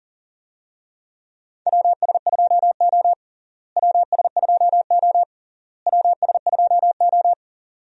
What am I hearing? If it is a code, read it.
Morse code, heard as a keyed tone, about 40 wpm, 700 Hz. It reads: WS2O WS2O WS2O
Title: USA Call Signs - 4 Characters - Repeated 3x - 40wpm